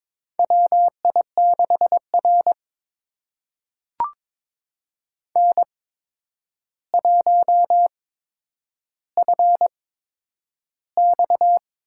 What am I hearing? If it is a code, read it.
Morse code, heard as a keyed tone, about 23 wpm, spaced as for 10 wpm, 700 Hz. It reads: WI6R N1FX
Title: USA Call Signs - 4 Characters - 8x Character Spacing - 22wpm